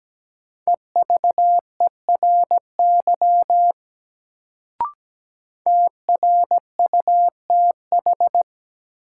Morse code, heard as a keyed tone, about 17 wpm, 700 Hz. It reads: EVERY TRUTH
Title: Sets of 5 Words - New Word Drill - 17wpm